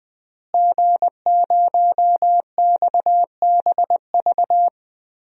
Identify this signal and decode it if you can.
Morse code, heard as a keyed tone, about 20 wpm, 700 Hz. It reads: G0XBV
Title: DXCC-100 Call Signs - 5 Characters - Repeated 3x - 20wpm